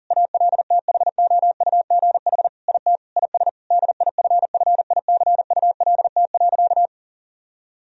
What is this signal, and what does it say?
Morse code, heard as a keyed tone, about 40 wpm, 700 Hz. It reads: ALTHOUGH IT IS DIFFICULT.
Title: Sentences from Top 1000 Words - Review Part 3 - 40wpm